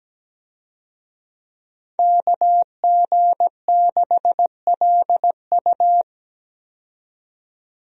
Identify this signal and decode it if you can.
Morse code, heard as a keyed tone, about 17 wpm, 700 Hz. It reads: KG6LU
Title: USA Call Signs - 5 Characters - 17wpm